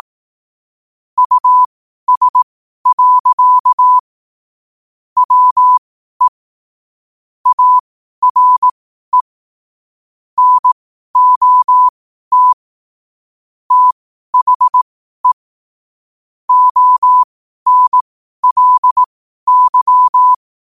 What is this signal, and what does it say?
Morse code, heard as a keyed tone, about 18 wpm, spaced as for 12 wpm, 1000 Hz. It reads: US. WE ARE NOT THE ONLY
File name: qotdfiles / 12wpm